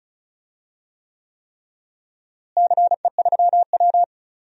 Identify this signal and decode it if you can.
Morse code, heard as a keyed tone, about 36 wpm, 700 Hz. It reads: CE3W
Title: DXCC-100 Call Signs - 4 Characters - Repeated 3x - 35wpm